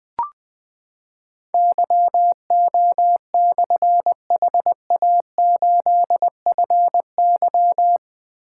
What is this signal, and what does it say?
Morse code, heard as a keyed tone, about 20 wpm, 700 Hz. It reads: YO/HA8FY
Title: Difficult Call Signs 20wpm